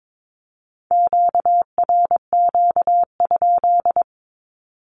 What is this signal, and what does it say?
Morse code, heard as a keyed tone, about 22 wpm, 700 Hz. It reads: QRQ?
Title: Q-Codes with Spoken Meaning - 22wpm